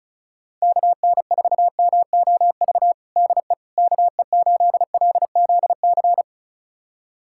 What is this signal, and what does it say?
Morse code, heard as a keyed tone, about 34 wpm, 700 Hz. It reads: KN4MOV DE KE8LZC
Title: USA Callsign DE Callsign - 6 Characters - 35wpm